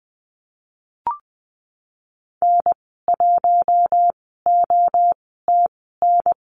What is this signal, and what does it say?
Morse code, heard as a keyed tone, about 20 wpm, spaced as for 15 wpm, 700 Hz. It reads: N1OTN
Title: 042 Lesson 8g - Number 1 - Call Signs 20wpm